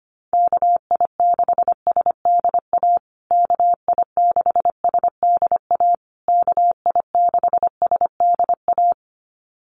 Morse code, heard as a keyed tone, about 25 wpm, 700 Hz. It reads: KI6SDA KI6SDA KI6SDA
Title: USA Call Signs Repeated 3x 25wpm